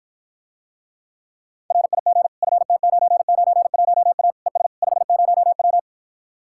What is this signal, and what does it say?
Morse code, heard as a keyed tone, about 54 wpm, 700 Hz. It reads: RIG FT991A ES 50W